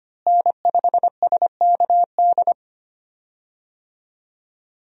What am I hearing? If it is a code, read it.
Morse code, heard as a keyed tone, about 25 wpm, 700 Hz. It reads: N5SKD
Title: USA Call Signs - 5 Characters - Repeated 2x - 25wpm